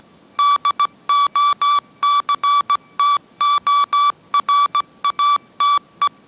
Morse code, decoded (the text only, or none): DOCTORATE